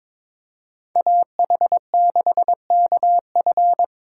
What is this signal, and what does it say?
Morse code, heard as a keyed tone, about 26 wpm, 700 Hz. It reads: AH6KF
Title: Speed Racing - USA 2x2 Call Signs 22wpm